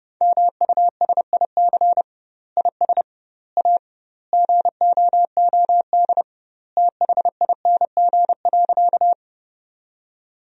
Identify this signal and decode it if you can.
Morse code, heard as a keyed tone, about 32 wpm, 700 Hz. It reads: MUSIC IS A GOOD THING.
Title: 2x Word-Spacing - Sentences from Top 300 Words - 30wpm